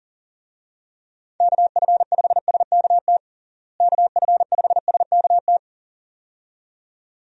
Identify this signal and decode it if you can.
Morse code, heard as a keyed tone, about 40 wpm, 700 Hz. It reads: KF5SKT KF5SKT
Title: USA Call Signs - 6 Characters - Repeated 2x - 40wpm